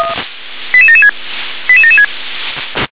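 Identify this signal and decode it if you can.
Five-tone selective call (ZVEI), heard as quick runs of five tones, tones of about 70 ms. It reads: 88776 88776
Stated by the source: ZVEI five-tone selective call 88761, SDR recording.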